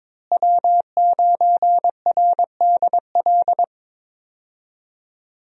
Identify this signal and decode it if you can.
Morse code, heard as a keyed tone, about 22 wpm, 700 Hz. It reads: W9RDL
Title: USA Call Signs - 5 Characters - Repeated 2x - 22wpm